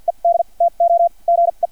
Morse code, heard as a keyed tone, about 50 wpm, spaced as for 38 wpm, 680 Hz, with radio noise. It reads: ED TO GE
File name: archived / CRSnet